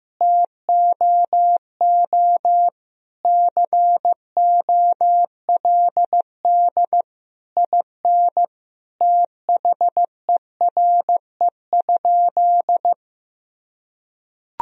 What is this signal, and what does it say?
Morse code, heard as a keyed tone, about 15 wpm, 700 Hz. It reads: TOO COLD IN THERE?
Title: Sentences from Top 1000 Words - Review Part 4 - 15wpm